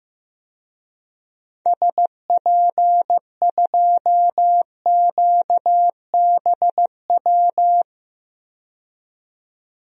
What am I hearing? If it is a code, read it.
Morse code, heard as a keyed tone, about 15 wpm, 700 Hz. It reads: SP2QBW
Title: DXCC-100 Call Signs - 6 Characters - 15wpm